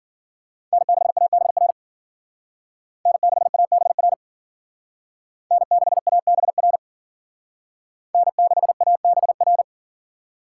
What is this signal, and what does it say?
Morse code, heard as a keyed tone, about 52 wpm, 700 Hz. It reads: N6ABR N6ABR N6ABR N6ABR